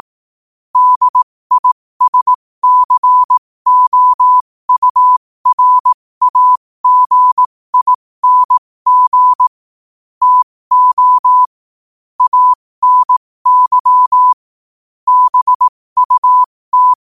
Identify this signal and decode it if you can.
Morse code, heard as a keyed tone, about 18 wpm, spaced as for 15 wpm, 1000 Hz. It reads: DISCOURAGING TO ANY BUT
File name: qotdfiles / 15wpm